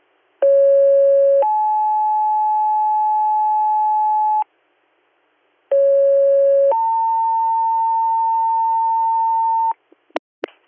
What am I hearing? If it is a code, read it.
Two-tone page, heard as a one-second tone, then a three-second tone, about 550 then 860 Hz, about 550 then 900 Hz, with noise. It is heard from 0.4 to 9.7 s.